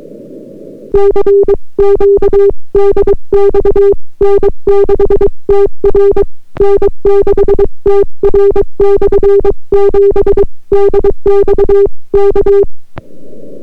Morse code, heard as a keyed tone, about 22 wpm, 400 Hz. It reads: CQ DX N6TR N6TR/7 DX K